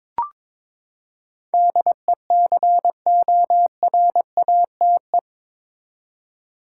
Morse code, heard as a keyed tone, about 22 wpm, 700 Hz. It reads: DECORATE
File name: practice / 22wpm / Eight Letter Words - 22wpm